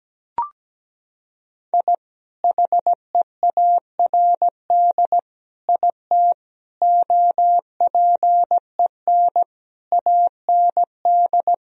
Morse code, heard as a keyed tone, about 17 wpm, 700 Hz. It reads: I HEARD IT OPEN AND
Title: Sentences from Top 1000 Words - Review Part 1 - 17wpm